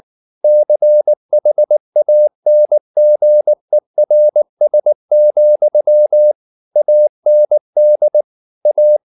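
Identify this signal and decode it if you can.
Morse code, heard as a keyed tone, about 19 wpm, 600 Hz. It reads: CHANGERS, AND A
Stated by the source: Morse code at 19 wpm from CW med Gnister, reading Ken Follett's novel Never in English.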